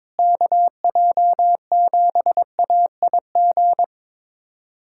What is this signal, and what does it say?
Morse code, heard as a keyed tone, about 22 wpm, 700 Hz. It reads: KJ7AIG
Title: USA Callsign DE Callsign - 6 Characters - 22wpm